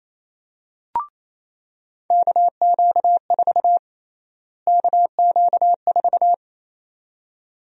Morse code, heard as a keyed tone, about 27 wpm, 700 Hz. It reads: KQ4 KQ4
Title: USA Call Sign Prefixes - Repeated 2x - 28wpm